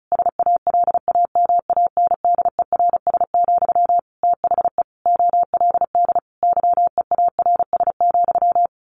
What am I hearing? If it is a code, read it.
Morse code, heard as a keyed tone, about 34 wpm, 700 Hz. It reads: SALAMANDERS, THE OLD YEARS,
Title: A Sound Of Thunder 35wpm